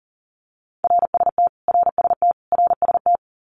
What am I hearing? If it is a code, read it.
Morse code, heard as a keyed tone, about 40 wpm, 700 Hz. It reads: RST RST RST
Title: Abbreviations Repeated 3x - 40wpm